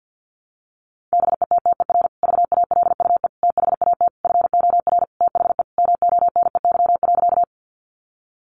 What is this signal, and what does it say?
Morse code, heard as a keyed tone, about 50 wpm, 700 Hz. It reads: BETTER VALUE THAT FOR THE MONEY.
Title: Sentences from Top 600 Words 50wpm